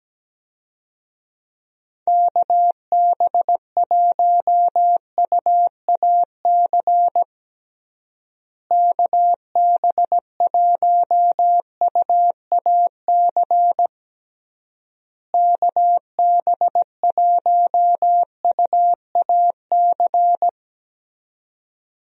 Morse code, heard as a keyed tone, about 17 wpm, 700 Hz. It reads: KB1UAC KB1UAC KB1UAC
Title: USA Call Signs - 6 Characters - Repeated 3x - 17wpm